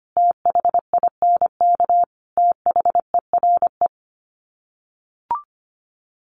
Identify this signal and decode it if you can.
Morse code, heard as a keyed tone, about 25 wpm, 700 Hz. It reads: THINK THERE
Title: Sets of 3 Words 25wpm